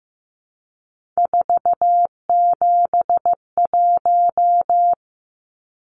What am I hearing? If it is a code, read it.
Morse code, heard as a keyed tone, about 15 wpm, 700 Hz. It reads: 471
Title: Numbers - 3-Digit 15wpm